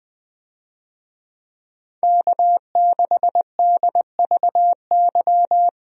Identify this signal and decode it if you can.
Morse code, heard as a keyed tone, about 20 wpm, 700 Hz. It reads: K6DVY
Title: USA Call Signs - 5 Characters - Repeated 3x - 20wpm